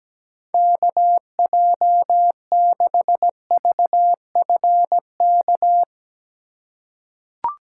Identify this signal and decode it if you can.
Morse code, heard as a keyed tone, about 17 wpm, 700 Hz. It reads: KJ6VFK
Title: USA Call Signs - 6 Characters - 4x Character Spacing - 17wpm